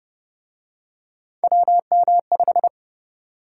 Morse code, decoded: WM5

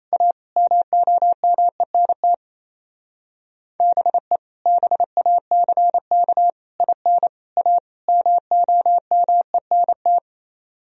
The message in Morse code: A MOMENT BE BACK IN A MOMENT

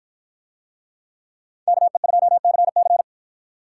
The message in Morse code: KE2CC

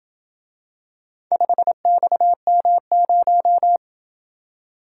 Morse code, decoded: 5XM0

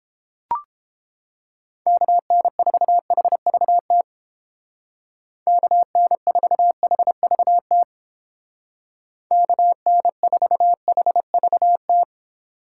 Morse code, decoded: KN4HVT KN4HVT KN4HVT

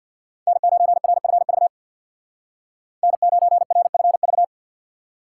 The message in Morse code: N9RFV N9RFV

